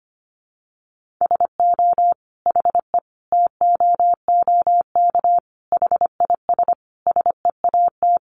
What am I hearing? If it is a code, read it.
Morse code, heard as a keyed tone, about 25 wpm, 700 Hz. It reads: SO HE TOOK HIS SEAT
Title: Sentences from Top 1500 Words 25wpm